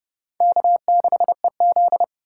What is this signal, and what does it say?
Morse code, heard as a keyed tone, about 36 wpm, 700 Hz. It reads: K6EZ
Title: Speed Racing - USA 1x2 Call Signs 30wpm